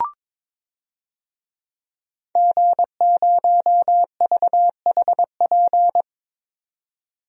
Morse code, heard as a keyed tone, about 22 wpm, 700 Hz. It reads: G0VHP
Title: DXCC-100 Call Signs - 5 Characters - 22wpm